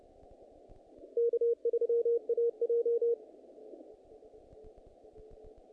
Morse code, heard as a keyed tone, about 30 wpm, 470 Hz. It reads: K3AJ
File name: contests / naqp-cw-k5tr-op-k5pi- -2000